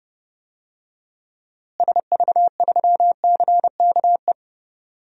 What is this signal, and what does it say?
Morse code, heard as a keyed tone, about 32 wpm, 700 Hz. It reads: SV3CKE
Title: DXCC-100 Call Signs - 6 Characters - Repeated 3x - 30wpm